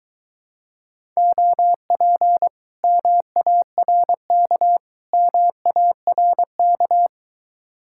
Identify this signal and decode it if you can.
Morse code, heard as a keyed tone, about 24 wpm, 700 Hz. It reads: OP MARK MARK